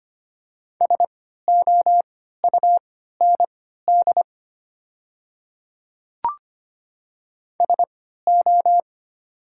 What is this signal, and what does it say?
Morse code, heard as a keyed tone, about 25 wpm, spaced as for 14 wpm, 700 Hz. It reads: SOUND SO
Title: 3x Character Spacing - Top 100 Words - 25wpm